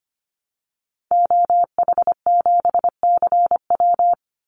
Morse code, heard as a keyed tone, about 25 wpm, 700 Hz. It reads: OH7CW
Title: Call Signs Repeated 2x - 25wpm